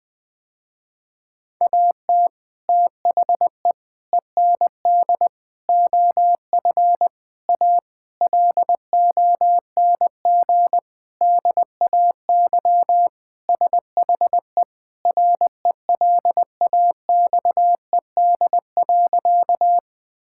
Morse code, heard as a keyed tone, about 20 wpm, 700 Hz. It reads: AT THE END OF A LONG DAY SHE RELAXED.
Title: Sentences with 4-Word Phrases - Spoken Beginning - 20wpm